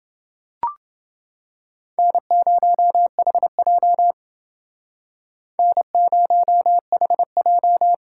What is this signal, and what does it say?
Morse code, heard as a keyed tone, about 29 wpm, 700 Hz. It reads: N0HJ N0HJ